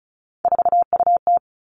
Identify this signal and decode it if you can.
Morse code, heard as a keyed tone, about 36 wpm, 700 Hz. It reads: 4UT
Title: Three Challenging Letter-Number Combinations 35wpm